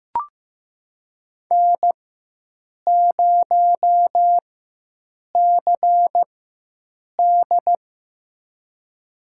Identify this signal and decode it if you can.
Morse code, heard as a keyed tone, about 15 wpm, spaced as for 7 wpm, 700 Hz. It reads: N0CD
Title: USA Call Signs - 4 Characters - 4x Character Spacing - 15wpm